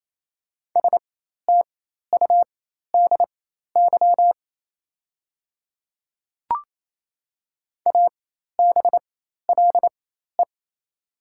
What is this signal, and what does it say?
Morse code, heard as a keyed tone, about 28 wpm, spaced as for 13 wpm, 700 Hz. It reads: STUDY ABLE
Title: 4x Character Spacing - Top 500 Words - 28wpm